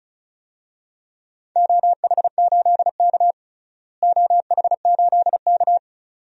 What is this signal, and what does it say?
Morse code, heard as a keyed tone, about 36 wpm, 700 Hz. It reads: OH8K OH8K
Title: DXCC-100 Call Signs - 4 Characters - Repeated 2x - 35wpm